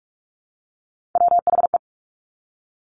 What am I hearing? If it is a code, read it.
Morse code, heard as a keyed tone, about 46 wpm, 700 Hz. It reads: WHE
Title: Top Three Letter Combinations in Words 45wpm